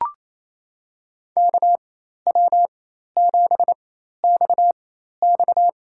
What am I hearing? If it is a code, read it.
Morse code, heard as a keyed tone, about 28 wpm, spaced as for 13 wpm, 700 Hz. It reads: KW7XX
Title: USA Call Signs - 5 Characters - 4x Character Spacing - 28wpm